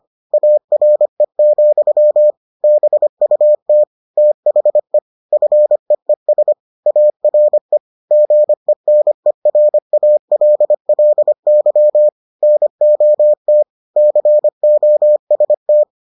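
Morse code, heard as a keyed tone, about 25 wpm, 600 Hz. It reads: ARE, BUT THE FEES ARE GENERALLY NOT COST
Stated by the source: Morse edition of the Foundations of Amateur Radio episodes.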